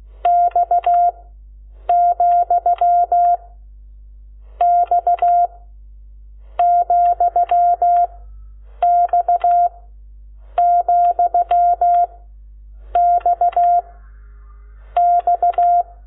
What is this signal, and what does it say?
Morse code, heard as a keyed tone, about 16 wpm, 700 Hz. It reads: X , X , X , X X
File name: Lecciones / CLASE Nº12